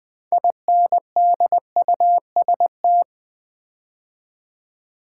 Morse code, heard as a keyed tone, about 20 wpm, 700 Hz. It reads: INDUST
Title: Build A Word - 2 Letters At A Time - 8-Letter Words - 20wpm